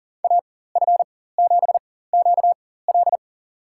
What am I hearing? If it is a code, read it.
Morse code, heard as a keyed tone, about 40 wpm, spaced as for 19 wpm, 700 Hz. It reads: AF7QL
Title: USA Call Signs - 5 Characters - 4x Character Spacing - 40wpm